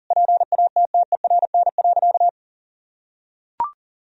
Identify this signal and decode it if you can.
Morse code, heard as a keyed tone, about 40 wpm, 700 Hz. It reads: PATTERN.
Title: Sentences from Top 1000 Words - Review Part 2 - 40wpm